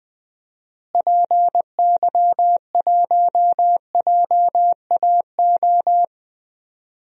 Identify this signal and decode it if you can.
Morse code, heard as a keyed tone, about 21 wpm, 700 Hz. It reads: PY1JAO
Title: DXCC-100 Call Signs - 6 Characters - Repeated 2x - 20wpm